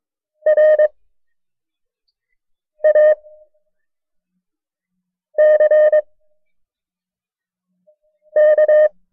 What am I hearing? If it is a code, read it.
Morse code, heard as a keyed tone, about 22 wpm, spaced as for 4 wpm, 610 Hz. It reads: RACK
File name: FreeFiles / CW-practice / KB6MT-tape1-side2